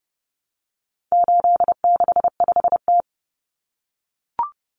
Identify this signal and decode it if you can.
Morse code, heard as a keyed tone, about 30 wpm, 700 Hz. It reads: 865T
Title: Common Cut Numbers - 4-Digit 30wpm